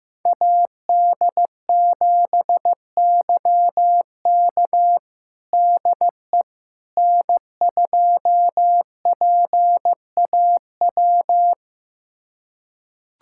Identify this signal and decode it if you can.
Morse code, heard as a keyed tone, about 15 wpm, 700 Hz. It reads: AD7YK DE N2PAW
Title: USA Callsign DE Callsign - 5 Characters - 15wpm